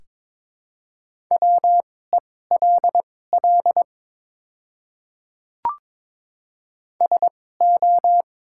Morse code, heard as a keyed tone, about 23 wpm, spaced as for 16 wpm, 700 Hz. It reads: WELL SO
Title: 2x Character Spacing - Top 200 Words - 22wpm